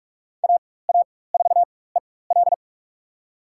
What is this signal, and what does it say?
Morse code, heard as a keyed tone, about 42 wpm, spaced as for 21 wpm, 700 Hz. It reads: AA4EL
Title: USA Call Signs - 5 Characters - 4x Character Spacing - 45wpm